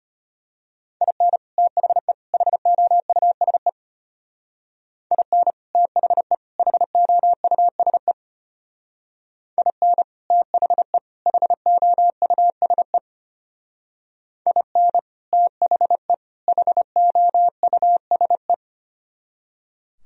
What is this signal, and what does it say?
Morse code, heard as a keyed tone, about 32 wpm, 700 Hz. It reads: IN THE HOUSE IN THE HOUSE IN THE HOUSE IN THE HOUSE